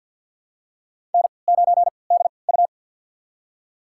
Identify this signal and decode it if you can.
Morse code, heard as a keyed tone, about 52 wpm, spaced as for 28 wpm, 700 Hz. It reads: N9DU
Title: USA Call Signs - 4 Characters - 3x Character Spacing - 50wpm